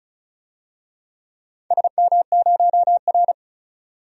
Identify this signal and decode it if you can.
Morse code, heard as a keyed tone, about 36 wpm, 700 Hz. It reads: SM0R
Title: DXCC-100 Call Signs - 4 Characters - 35wpm